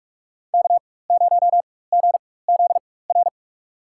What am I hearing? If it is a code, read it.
Morse code, heard as a keyed tone, about 44 wpm, spaced as for 21 wpm, 690 Hz. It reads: K0GZR
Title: USA Call Signs - 4x Character Spacing - 45wpm